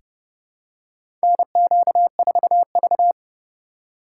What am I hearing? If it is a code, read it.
Morse code, heard as a keyed tone, about 30 wpm, 700 Hz. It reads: NQ4V
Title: USA Call Signs - 4 Characters - 2x Character Spacing - 30wpm